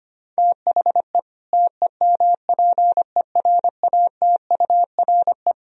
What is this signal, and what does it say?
Morse code, heard as a keyed tone, about 25 wpm, 700 Hz. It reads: THE TEMPERATURE
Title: Sentences from Top 1000 Words - Review Part 4 - 25wpm